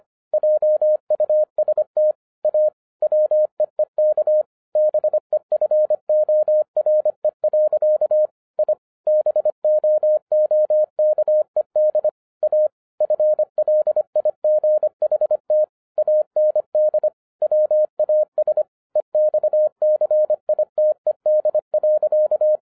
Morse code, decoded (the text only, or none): JUST A WEEK BEFORE. I BOOKED A FLIGHT AND WAS EXCITED.